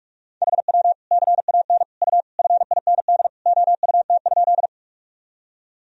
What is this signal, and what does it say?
Morse code, heard as a keyed tone, about 46 wpm, 700 Hz. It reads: HW CAN U FIND OUT?
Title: Sentences from Top 200 Words and Abbreviations 45wpm